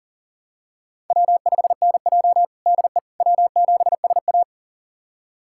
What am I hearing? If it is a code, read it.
Morse code, heard as a keyed tone, about 40 wpm, 700 Hz. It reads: W5NJ DE W7SA
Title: USA Callsign DE Callsign - 4 Characters - 40wpm